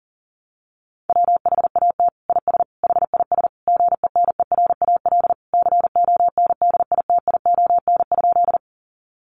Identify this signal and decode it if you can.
Morse code, heard as a keyed tone, about 40 wpm, 700 Hz. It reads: WHAT IS HIS GENERAL CONDITION?
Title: Sentences from Top 1000 Words 40wpm